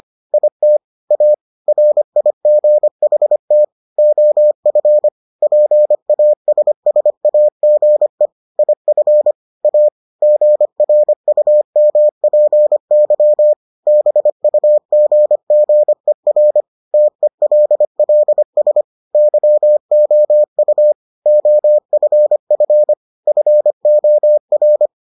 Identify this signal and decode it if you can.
Morse code, heard as a keyed tone, about 25 wpm, 600 Hz. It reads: IT A RIGHT OF PASSAGE IF A GRUMPY BUGGER TELLS YOU OFF FOR